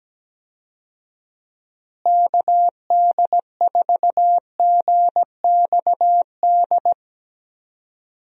Morse code, heard as a keyed tone, about 17 wpm, 700 Hz. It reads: KD4GXD
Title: USA Call Signs - 6 Characters - Repeated 2x - 17wpm